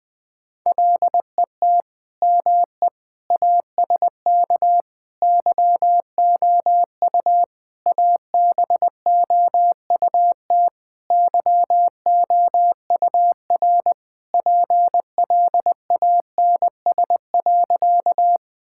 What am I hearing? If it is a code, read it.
Morse code, heard as a keyed tone, about 21 wpm, 700 Hz. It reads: LET ME ASK YOU ABOUT YOUR PLANS.
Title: Sentences with 5-Word Phrases - Spoken Beginning - 20wpm